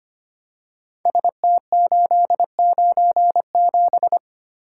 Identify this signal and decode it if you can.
Morse code, heard as a keyed tone, about 25 wpm, 700 Hz. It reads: ST897
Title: Code Groups - 5 Letter:Numbers - 25wpm